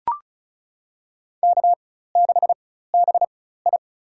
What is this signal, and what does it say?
Morse code, heard as a keyed tone, about 36 wpm, spaced as for 16 wpm, 700 Hz. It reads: K6BI
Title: USA Call Signs - 4 Characters - 4x Character Spacing - 35wpm